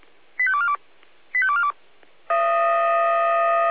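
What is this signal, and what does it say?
Five-tone selective call (ZVEI), heard as quick runs of five tones, tones of about 70 ms. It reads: 85242 85242
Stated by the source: ZVEI five-tone selective call received on an SDR.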